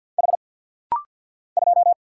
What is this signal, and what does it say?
Morse code, heard as a keyed tone, about 49 wpm, 700 Hz. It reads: H 2
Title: Single Letter-Number - Rapid-Fire 50wpm